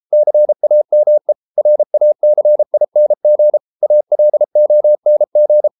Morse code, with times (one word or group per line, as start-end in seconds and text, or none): 0.1-1.3 CAME
1.6-3.6 RACING
3.8-5.7 ALONG